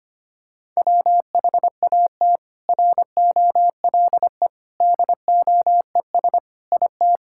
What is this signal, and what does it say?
Morse code, heard as a keyed tone, about 25 wpm, 700 Hz. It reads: WHAT ROLE DOES IT
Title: Sentences from Top 1000 Words - Review Part 6 - 25wpm